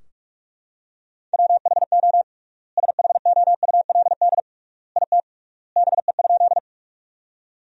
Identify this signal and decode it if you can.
Morse code, heard as a keyed tone, about 43 wpm, 700 Hz. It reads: WHO SHOULD IT BE?
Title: 3x Word-Spacing - Sentences from Top 300 Words - 45wpm